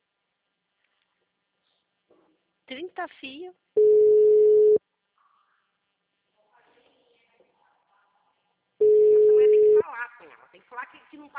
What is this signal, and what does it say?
Ringback tone (Europe, 425 Hz), with speech.